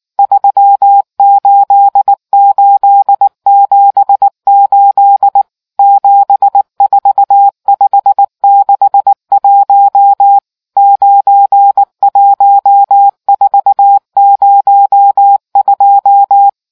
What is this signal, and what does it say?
Morse code, heard as a keyed tone, about 19 wpm, 800 Hz. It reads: 38878 74561 91402